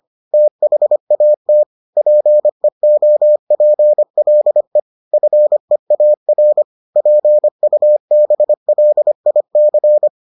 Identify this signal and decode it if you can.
Morse code, heard as a keyed tone, about 25 wpm, 600 Hz. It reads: THAT PEOPLE FEAR PUBLIC